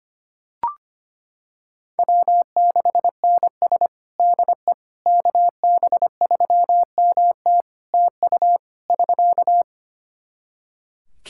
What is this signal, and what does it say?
Morse code, heard as a keyed tone, about 25 wpm, 700 Hz. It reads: W6NS DE KB3MT TU <SK>
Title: Practice QSO - Exchange 4 - Ending - 25wpm